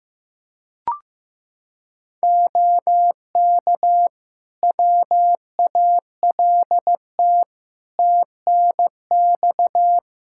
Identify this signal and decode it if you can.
Morse code, heard as a keyed tone, about 15 wpm, 700 Hz. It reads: OK WALT TNX
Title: Practice QSO - Exchange 4 - Ending - 15wpm